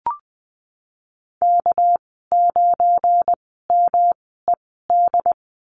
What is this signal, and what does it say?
Morse code, heard as a keyed tone, about 20 wpm, spaced as for 14 wpm, 700 Hz. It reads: K9MED
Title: 207 Lesson 33g - Letter j - Call Signs 20wpm